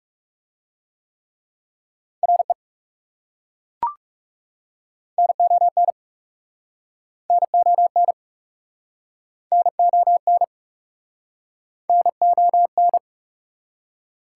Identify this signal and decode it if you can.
Morse code, heard as a keyed tone, about 41 wpm, 690 Hz. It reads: RE NON NON NON NON